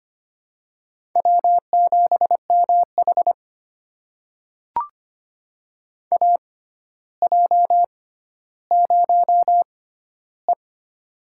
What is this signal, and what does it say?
Morse code, heard as a keyed tone, about 25 wpm, spaced as for 13 wpm, 700 Hz. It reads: W7MH AJ0E
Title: USA Call Signs - 4 Characters - 6x Character Spacing - 25wpm